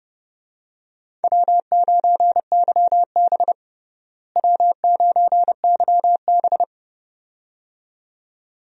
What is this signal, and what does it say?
Morse code, heard as a keyed tone, about 30 wpm, 700 Hz. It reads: W9YB W9YB
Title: USA Call Signs - 4 Characters - Repeated 2x - 30wpm